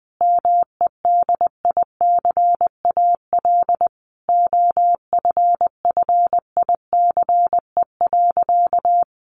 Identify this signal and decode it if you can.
Morse code, heard as a keyed tone, about 20 wpm, 700 Hz. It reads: MEDICAL OFFICE.
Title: Additional Easy Phrases 20wpm